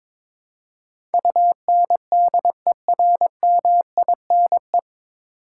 Morse code, heard as a keyed tone, about 22 wpm, 700 Hz. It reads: UNDERMINE
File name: 9 Letter Words - First 3 Letters Given - 22wpm